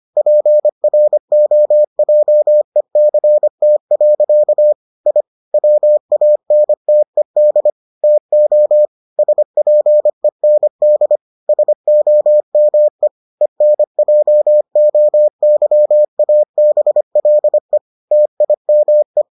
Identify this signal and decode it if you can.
Morse code, heard as a keyed tone, about 25 wpm, 600 Hz. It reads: PROJECT. I WANTED TO SPEND SOME ENJOYABLE TIME